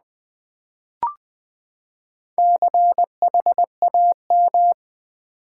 Morse code, heard as a keyed tone, about 20 wpm, 700 Hz. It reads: CHAM
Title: Build A Word - 4 Letters At A Time - 12-Letter Words - 20wpm